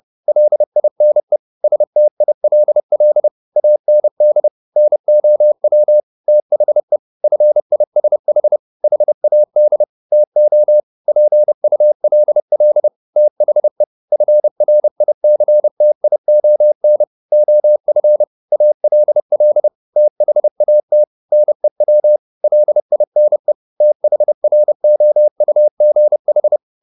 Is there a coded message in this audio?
Yes: Morse code from 0.3 to 26.6 s, heard as a keyed tone, about 30 wpm, 600 Hz. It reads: LINE STILL AND NOW THE FISH HAD TO PULL THE FRICTION OF ALL THAT NEW LINE THROUGH